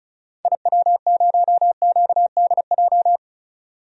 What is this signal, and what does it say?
Morse code, heard as a keyed tone, about 36 wpm, 700 Hz. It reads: IW0QDJ
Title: DXCC-100 Call Signs - 6 Characters - 35wpm